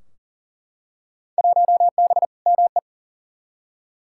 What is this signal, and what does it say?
Morse code, heard as a keyed tone, about 40 wpm, 700 Hz. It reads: 1B ME
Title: Field Day Exchanges - 40wpm